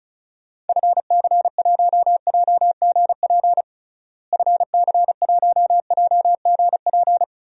Morse code, heard as a keyed tone, about 36 wpm, 700 Hz. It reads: FC1JGP FC1JGP
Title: DXCC-100 Call Signs - 6 Characters - Repeated 3x - 35wpm